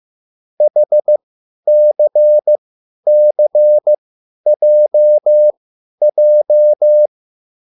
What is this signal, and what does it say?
Morse code, heard as a keyed tone, about 15 wpm, spaced as for 10 wpm, 600 Hz. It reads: HCCJJ
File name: QSL / CW lesson